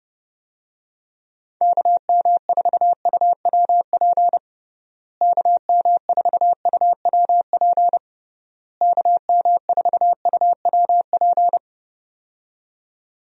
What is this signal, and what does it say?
Morse code, heard as a keyed tone, about 30 wpm, 700 Hz. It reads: KM4UWP KM4UWP KM4UWP
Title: USA Call Signs - 6 Characters - Repeated 3x - 30wpm